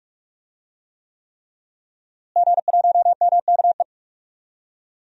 Morse code, heard as a keyed tone, about 47 wpm, 700 Hz. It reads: G1MKE
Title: DXCC-100 Call Signs - 5 Characters - Repeated 3x - 45wpm